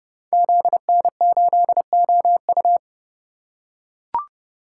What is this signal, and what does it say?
Morse code, heard as a keyed tone, about 30 wpm, 700 Hz. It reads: ZN8OU
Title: Code Groups - 5 Letter:Numbers - 30wpm